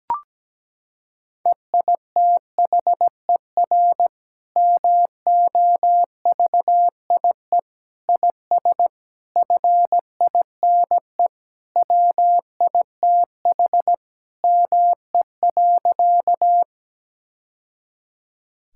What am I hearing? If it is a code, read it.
Morse code, heard as a keyed tone, about 17 wpm, 700 Hz. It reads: EITHER MOVIE IS FINE WITH ME.
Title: Sentences from Top 1000 Words - Review Part 5 - 17wpm